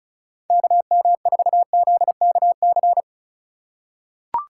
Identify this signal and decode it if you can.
Morse code, heard as a keyed tone, about 36 wpm, 700 Hz. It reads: KM4ZKC
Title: USA Call Signs - 6 Characters - 2x Character Spacing - 35wpm